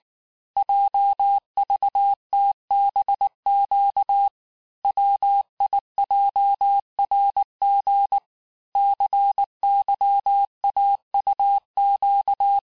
Morse code, heard as a keyed tone, about 19 wpm, 800 Hz. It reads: JVTBQ WIJRG CYAUQ
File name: cours-du-lundi files / cours-1589-texte-chiffre-vitesse-18mpm